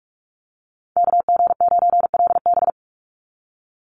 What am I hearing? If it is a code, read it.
Morse code, heard as a keyed tone, about 44 wpm, 700 Hz. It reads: KG9LB
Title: Call Signs Repeated 2x - 45wpm